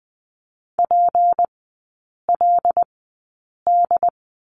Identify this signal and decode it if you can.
Morse code, heard as a keyed tone, about 20 wpm, spaced as for 8 wpm, 700 Hz. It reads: PLD
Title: 212 Lesson 34e - Number 8 - Three Characters 20wpm